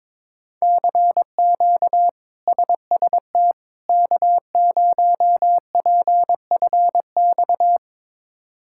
Morse code, heard as a keyed tone, about 23 wpm, 700 Hz. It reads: CQ SST K0PFX
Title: K1USN SST QSO Practice - 22wpm